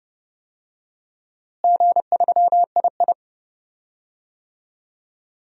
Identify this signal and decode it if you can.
Morse code, heard as a keyed tone, about 31 wpm, 700 Hz. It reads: G3II
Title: DXCC-100 Call Signs - 4 Characters - Repeated 3x - 30wpm